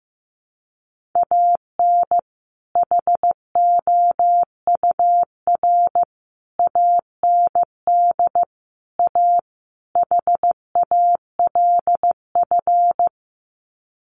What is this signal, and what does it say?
Morse code, heard as a keyed tone, about 15 wpm, 700 Hz. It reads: AN HOUR AND A HALF
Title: Sets of 5 Words 15wpm